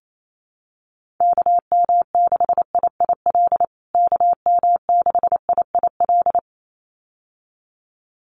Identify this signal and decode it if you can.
Morse code, heard as a keyed tone, about 28 wpm, 700 Hz. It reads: KM6IIL KM6IIL
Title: USA Call Signs Repeated 2x 28wpm